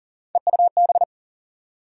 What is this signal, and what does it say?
Morse code, heard as a keyed tone, about 40 wpm, 700 Hz. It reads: EUB
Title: USA Call Sign Postfixes - 3 Letters 40wpm